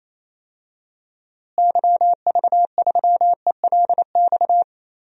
Morse code, heard as a keyed tone, about 28 wpm, 700 Hz. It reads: YV3ELX
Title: DXCC-100 Call Signs - 6 Characters - Repeated 2x - 28wpm